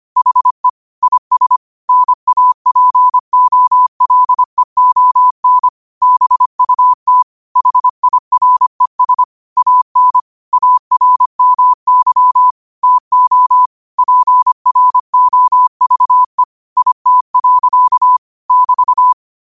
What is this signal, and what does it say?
Morse code, heard as a keyed tone, about 25 wpm, 1000 Hz. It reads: HE IS NAPOLEON BUT HIRES AN ARMY TO PROVE IT. <BT>
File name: qotdfiles / 25wpm